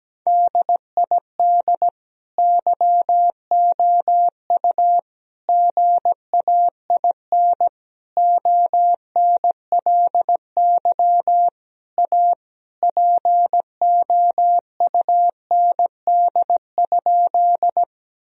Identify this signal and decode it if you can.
Morse code, heard as a keyed tone, about 17 wpm, 700 Hz. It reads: DID YOU GAIN ONLY A POUND?
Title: Sentences from Top 1000 Words - Review Part 2 - 17wpm